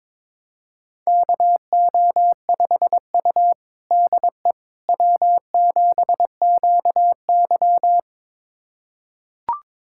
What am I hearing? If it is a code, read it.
Morse code, heard as a keyed tone, about 22 wpm, 700 Hz. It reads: KO5U DE W7QY
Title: USA Callsign DE Callsign - 4 Characters - 22wpm